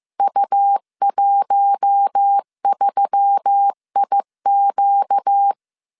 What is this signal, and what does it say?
Morse code, heard as a keyed tone, about 15 wpm, 800 Hz. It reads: U13IQ